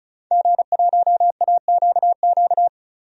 Morse code, heard as a keyed tone, about 36 wpm, 700 Hz. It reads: G1AQQ
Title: DXCC-100 Call Signs - 5 Characters - 35wpm